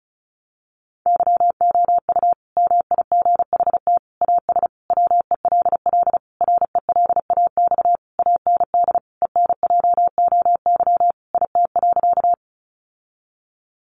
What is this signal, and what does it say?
Morse code, heard as a keyed tone, about 34 wpm, 700 Hz. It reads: YOU MIGHT AS WELL RELAX AND ENJOY IT.
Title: Sentences from Top 1400 Words 35wpm